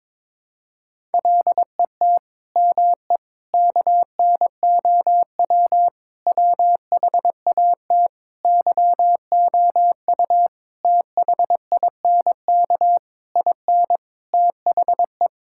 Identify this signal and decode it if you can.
Morse code, heard as a keyed tone, about 22 wpm, 700 Hz. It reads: LET ME KNOW WHAT YOU THINK IN THE
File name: Sets of 9 Words - 22wpm